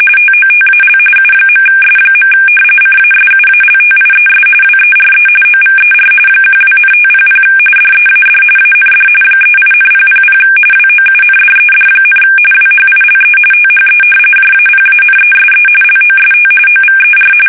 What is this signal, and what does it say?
RTTY, about 45.45 baud. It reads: TOTAL U.S. CASUALTIES WERE OVER 12,500 DEAD AND 38,000 WOUNDED, WHILE THE JAPANESE LOST OVER
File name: fsk850